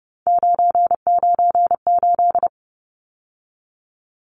Morse code, heard as a keyed tone, about 30 wpm, 700 Hz. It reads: 998
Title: Numbers - 3-Digit 30wpm